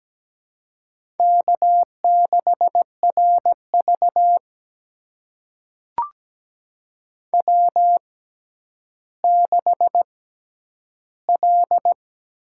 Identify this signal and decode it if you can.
Morse code, heard as a keyed tone, about 17 wpm, spaced as for 10 wpm, 700 Hz. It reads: K6RV W6L
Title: USA Call Signs - 4 Characters - 6x Character Spacing - 17wpm